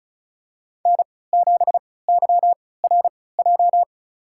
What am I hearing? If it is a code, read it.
Morse code, heard as a keyed tone, about 34 wpm, spaced as for 20 wpm, 700 Hz. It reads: N7YRJ
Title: USA Call Signs - 3x Character Spacing - 35wpm